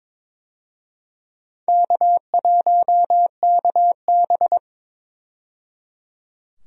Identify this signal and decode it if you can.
Morse code, heard as a keyed tone, about 22 wpm, 700 Hz. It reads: K1KB
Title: USA Call Signs - 4 Characters - 22wpm